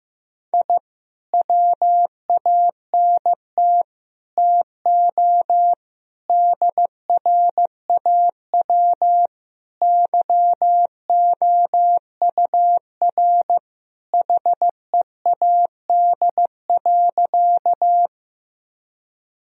Morse code, heard as a keyed tone, about 15 wpm, 700 Hz. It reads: I WANT TO DRAW YOUR HEAD.
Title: Sentences from Top 1000 Words - Review Part 1 - 15wpm